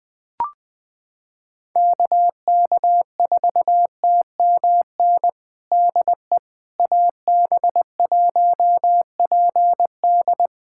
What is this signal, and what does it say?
Morse code, heard as a keyed tone, about 21 wpm, 700 Hz. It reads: KK4TMN DE AB1PD
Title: USA Callsign DE Callsign 20wpm